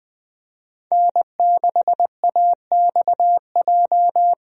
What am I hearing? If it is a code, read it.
Morse code, heard as a keyed tone, about 20 wpm, 700 Hz. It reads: N6AXJ
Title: USA Call Signs - 5 Characters - Repeated 2x - 20wpm